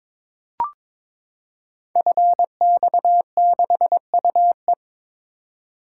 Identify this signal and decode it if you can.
Morse code, heard as a keyed tone, about 23 wpm, 700 Hz. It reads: FX6UE
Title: Code Groups - 5 Letter:Numbers - 22wpm